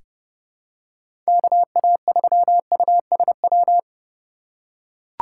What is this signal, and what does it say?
Morse code, heard as a keyed tone, about 32 wpm, 700 Hz. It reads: KA3USW
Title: USA Call Signs - 6 Characters - 3x Character Spacing - 30wpm